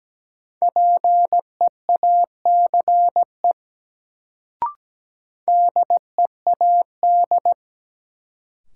Morse code, heard as a keyed tone, about 17 wpm, 700 Hz. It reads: PEACE DEAD
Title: Sentences from Top 1500 Words - New Word Drill - 17wpm